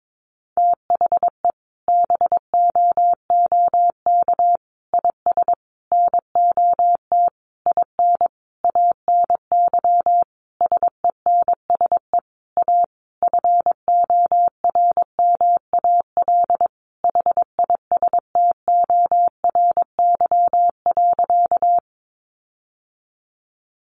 Morse code, heard as a keyed tone, about 22 wpm, 700 Hz. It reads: THE BOOK IS NOT IN ANY SENSE A FORMAL HISTORY.
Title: Sentences from Top 1300 Words 22wpm